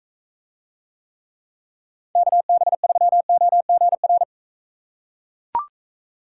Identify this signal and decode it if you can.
Morse code, heard as a keyed tone, about 44 wpm, 700 Hz. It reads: KB3OGR